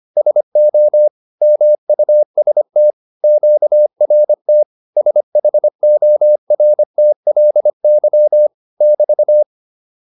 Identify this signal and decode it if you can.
Morse code, heard as a keyed tone, about 25 wpm, 600 Hz. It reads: SO MUST QRT SHORTLY <BT>